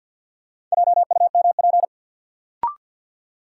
Morse code, heard as a keyed tone, about 53 wpm, 700 Hz. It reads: JUMP
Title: Sentences from Top 1300 Words - New Word Drill Alt - 50wpm